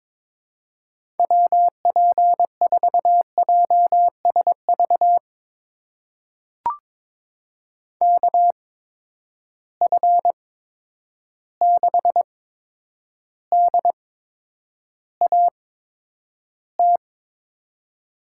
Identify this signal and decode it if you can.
Morse code, heard as a keyed tone, about 23 wpm, spaced as for 10 wpm, 700 Hz. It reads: WP4JSV KF6DAT
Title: USA Call Signs - 6 Characters - 8x Character Spacing - 22wpm